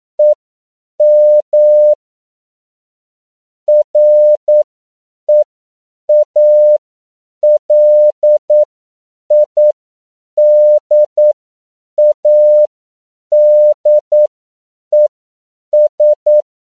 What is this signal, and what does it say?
Morse code, heard as a keyed tone, about 9 wpm, spaced as for 7 wpm, 600 Hz. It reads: EM REALIDADES
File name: CW / 10PPM